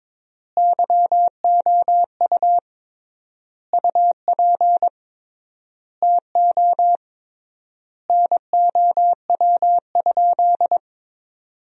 Morse code, heard as a keyed tone, about 22 wpm, 700 Hz. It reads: YOU UP TO NOW?
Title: 3x Word-Spacing - Sentences from Top 100 Words - 22wpm